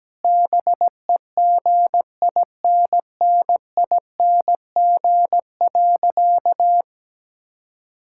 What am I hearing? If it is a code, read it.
Morse code, heard as a keyed tone, about 17 wpm, 700 Hz. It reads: BEGINNING.
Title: Sentences from Top 1000 Words - Review Part 5 - 17wpm